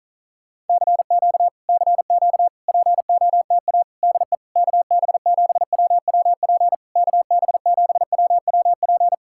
Morse code, heard as a keyed tone, about 46 wpm, 700 Hz. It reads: CQ CQ POTA DE KB7WWP KB7WWP